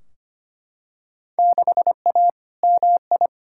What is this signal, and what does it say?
Morse code, heard as a keyed tone, about 25 wpm, 700 Hz. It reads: 6A MI
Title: Field Day Exchanges - 25wpm